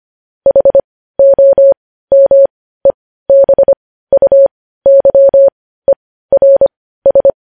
Morse code, heard as a keyed tone, about 25 wpm, spaced as for 15 wpm, 550 Hz. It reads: HOMEBUYERS
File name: podcast / News headlines-1x-25x15